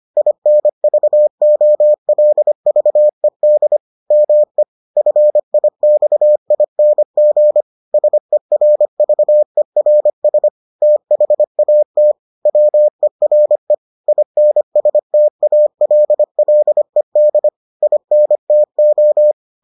Morse code, heard as a keyed tone, about 25 wpm, 600 Hz. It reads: INVOLVED ME FIXING SERVERS THAT WERE INSTALLED INTO